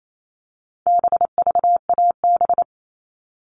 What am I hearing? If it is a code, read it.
Morse code, heard as a keyed tone, about 28 wpm, 700 Hz. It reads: BVAB
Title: All Cut Numbers - 4-Digit 28wpm